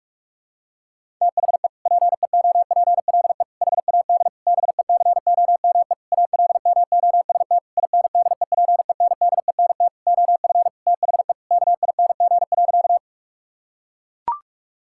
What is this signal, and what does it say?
Morse code, heard as a keyed tone, about 41 wpm, 700 Hz. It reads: THE PEOPLE HAD BECOME ALMOST INDEPENDENT OF THE KING.
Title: Sentences from Top 1000 Words - Review Part 6 - 45wpm